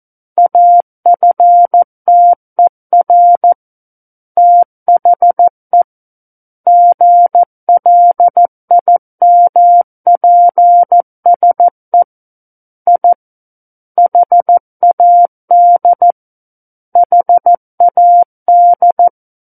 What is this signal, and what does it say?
Morse code, heard as a keyed tone, about 14 wpm, 700 Hz. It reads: AFTER THE GLIMPSE I HAD HAD
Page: War of the Worlds - 05-Chapter 5 - 14 WPM